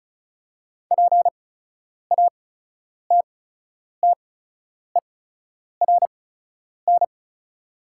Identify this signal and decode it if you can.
Morse code, heard as a keyed tone, about 36 wpm, spaced as for 10 wpm, 690 Hz. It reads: PATTERN
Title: 8x Character Spacing - Top 500 Words - 35wpm